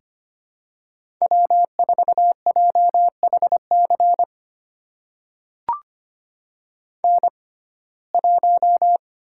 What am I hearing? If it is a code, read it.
Morse code, heard as a keyed tone, about 25 wpm, spaced as for 18 wpm, 700 Hz. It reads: W4JHC N1
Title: USA Call Signs - 5 Characters - 6x Character Spacing - 25wpm